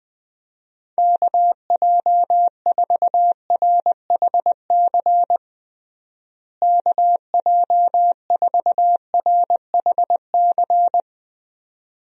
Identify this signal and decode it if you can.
Morse code, heard as a keyed tone, about 20 wpm, 700 Hz. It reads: KJ4RHC KJ4RHC
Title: USA Call Signs - 6 Characters - Repeated 2x - 20wpm